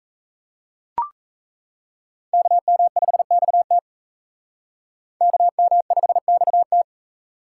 Morse code, heard as a keyed tone, about 35 wpm, 700 Hz. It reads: KM5XT KM5XT